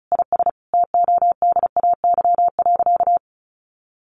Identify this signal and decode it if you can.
Morse code, heard as a keyed tone, about 34 wpm, 700 Hz. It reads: IS TODAY.
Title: Sentences from Top 1500 Words 35wpm